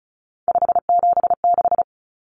Morse code, heard as a keyed tone, about 34 wpm, 700 Hz. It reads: 576
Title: Common Cut Numbers - 3-Digit 35wpm